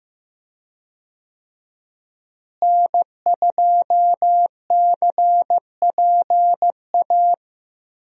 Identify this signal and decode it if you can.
Morse code, heard as a keyed tone, about 15 wpm, 700 Hz. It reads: N2CPA
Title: USA Call Signs - 5 Characters - Repeated 3x - 15wpm